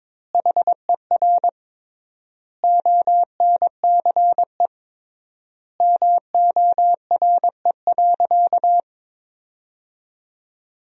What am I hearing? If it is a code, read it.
Morse code, heard as a keyed tone, about 22 wpm, 700 Hz. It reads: HER ONCE MORE.
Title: 3x Word-Spacing - Sentences from Top 500 Words - 22wpm